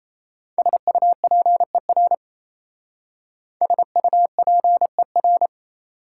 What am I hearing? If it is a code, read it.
Morse code, heard as a keyed tone, about 35 wpm, 700 Hz. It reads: SUPER SUPER